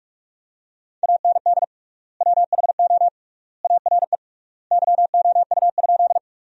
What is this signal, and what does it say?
Morse code, heard as a keyed tone, about 46 wpm, 700 Hz. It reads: AND WHO ARE YOU?
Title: 3x Word-Spacing - Sentences from Top 100 Words - 45wpm